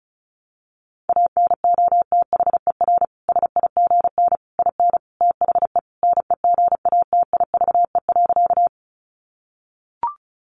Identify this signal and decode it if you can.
Morse code, heard as a keyed tone, about 34 wpm, 700 Hz. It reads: ANOTHER SIGN IN THE NEGATIVE.
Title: Sentences from Top 1000 Words 35wpm